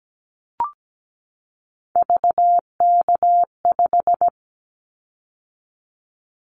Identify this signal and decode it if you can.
Morse code, heard as a keyed tone, about 17 wpm, 700 Hz. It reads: VK5
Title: Callsign Prefixes for Top 50 DXCC - 17wpm